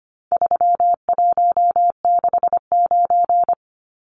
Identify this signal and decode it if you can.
Morse code, heard as a keyed tone, about 25 wpm, 700 Hz. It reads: 3169
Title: Numbers - 4-Digit 25wpm